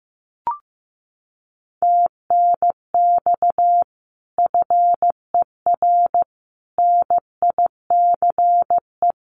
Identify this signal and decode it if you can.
Morse code, heard as a keyed tone, about 15 wpm, 700 Hz. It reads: TNX FER NICE
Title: Common QSO Phrases Encore 15wpm